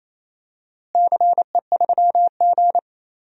Morse code, heard as a keyed tone, about 27 wpm, 700 Hz. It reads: CE3G
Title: DXCC-100 Call Signs - 4 Characters - Repeated 2x - 28wpm